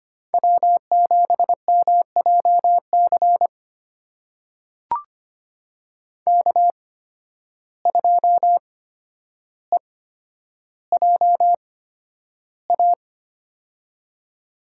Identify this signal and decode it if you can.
Morse code, heard as a keyed tone, about 25 wpm, spaced as for 11 wpm, 700 Hz. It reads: W7MJC K2EJA
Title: USA Call Signs - 5 Characters - 8x Character Spacing - 25wpm